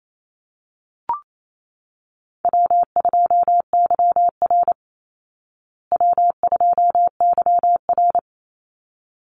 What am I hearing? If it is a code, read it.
Morse code, heard as a keyed tone, about 28 wpm, 700 Hz. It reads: W2YR W2YR
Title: Call Signs Repeated 3x - 28wpm